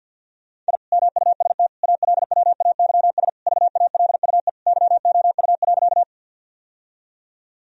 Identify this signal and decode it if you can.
Morse code, heard as a keyed tone, about 47 wpm, 700 Hz. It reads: I MUST ALWAYS VALUE YOU.
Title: Sentences from Top 1000 Words - Review Part 3 - 50wpm